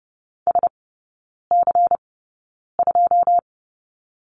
Morse code, heard as a keyed tone, about 30 wpm, spaced as for 9 wpm, 690 Hz. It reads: SC2
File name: Lesson 17e - Letter u - Three Characters 30wpm